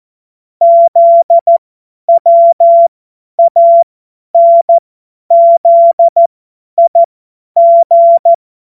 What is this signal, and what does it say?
Morse code, heard as a keyed tone, about 14 wpm, spaced as for 10 wpm, 680 Hz. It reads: ZWANZIG
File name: Morsekurs / Witze / Witz 20. Januar 2026 50 F70